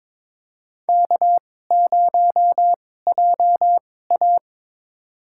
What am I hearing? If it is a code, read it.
Morse code, heard as a keyed tone, about 22 wpm, spaced as for 16 wpm, 700 Hz. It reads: K0JA
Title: USA Call Signs - 2x Character Spacing - 22wpm